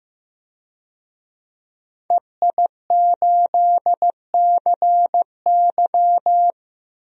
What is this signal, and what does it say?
Morse code, heard as a keyed tone, about 15 wpm, 700 Hz. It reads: EI8CY
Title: DXCC-100 Call Signs - 5 Characters - Repeated 3x - 15wpm